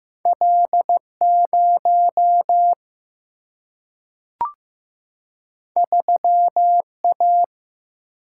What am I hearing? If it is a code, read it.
Morse code, heard as a keyed tone, about 15 wpm, 700 Hz. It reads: L 0 3 A
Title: Single-Letter Single-Number 15wpm